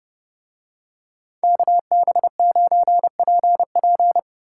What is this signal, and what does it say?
Morse code, heard as a keyed tone, about 29 wpm, 700 Hz. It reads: KB9PP